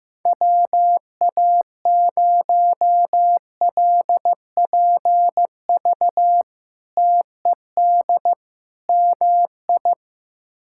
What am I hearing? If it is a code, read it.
Morse code, heard as a keyed tone, about 15 wpm, 700 Hz. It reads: WA0LPV TED MI
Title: K1USN SST QSO Practice - 15wpm